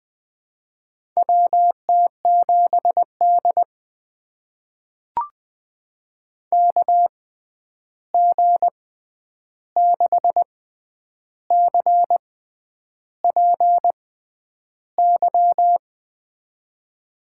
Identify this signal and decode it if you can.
Morse code, heard as a keyed tone, about 20 wpm, spaced as for 9 wpm, 700 Hz. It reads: WT7D KG6CPY
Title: USA Call Signs - 6x Character Spacing - 20wpm